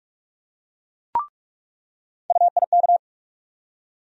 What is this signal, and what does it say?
Morse code, heard as a keyed tone, about 46 wpm, 700 Hz. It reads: UIK
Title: USA Call Sign Postfixes - 3 Letters 45wpm